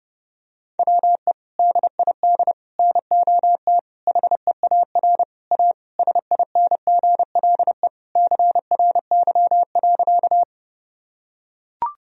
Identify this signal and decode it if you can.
Morse code, heard as a keyed tone, about 30 wpm, 700 Hz. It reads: WE DID NOT HEAR A SINGLE CRY.
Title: Sentences from Top 1000 Words - Review Part 4 - 30wpm